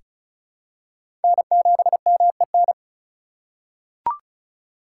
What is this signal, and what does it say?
Morse code, heard as a keyed tone, about 36 wpm, 700 Hz. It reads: N7MEN
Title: USA Call Signs - 4x Character Spacing - 35wpm